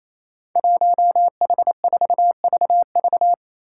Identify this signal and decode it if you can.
Morse code, heard as a keyed tone, about 28 wpm, 700 Hz. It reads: 1H4VV
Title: Five Challenging Letter-Number Combinations 28wpm